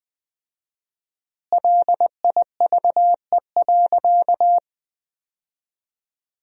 Morse code, heard as a keyed tone, about 21 wpm, 700 Hz. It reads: LIVE.
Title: 4x Word-Spacing - Sentences from Top 200 Words - 20wpm